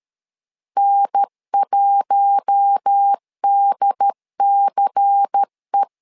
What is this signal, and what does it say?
Morse code, heard as a keyed tone, about 13 wpm, 800 Hz. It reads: N1DCE
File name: LEZIONI / 5 LBVC / 5 LBVC 12wpm 1955